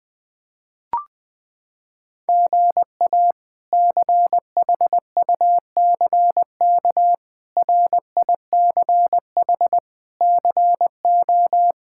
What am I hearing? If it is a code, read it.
Morse code, heard as a keyed tone, about 20 wpm, 700 Hz. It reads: GA CHUCK RICH CO
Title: K1USN SST QSO Practice - 20wpm